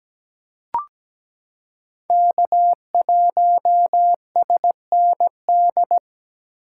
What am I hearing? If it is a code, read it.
Morse code, heard as a keyed tone, about 17 wpm, 700 Hz. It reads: K1SND
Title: K1USN SST QSO Practice - 17wpm